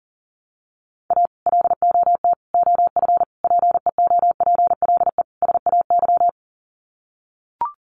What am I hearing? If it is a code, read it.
Morse code, heard as a keyed tone, about 40 wpm, 700 Hz. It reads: A LOT OF PEOPLE SAY
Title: Sets of 5 Words Encore2 40wpm